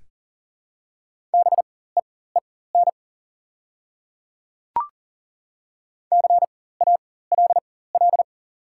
Morse code, heard as a keyed tone, about 40 wpm, spaced as for 19 wpm, 700 Hz. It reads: BEEN CALL
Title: 4x Character Spacing - Top 100 Words - 40wpm